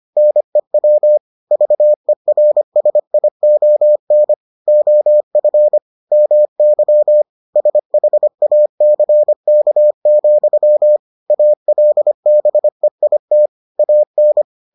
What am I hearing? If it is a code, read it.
Morse code, heard as a keyed tone, about 25 wpm, 600 Hz. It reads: NEW VERSION OF MY SHACK, ALBEIT AN